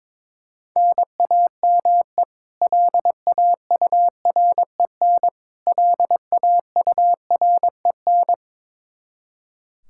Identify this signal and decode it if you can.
Morse code, heard as a keyed tone, about 22 wpm, 700 Hz. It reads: NAME LAUREN LAUREN
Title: Practice QSO - Exchange 1 - Beginning - 22wpm